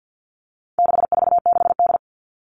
Morse code, heard as a keyed tone, about 50 wpm, 700 Hz. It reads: 646D
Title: All Cut Numbers - 4-Digit 50wpm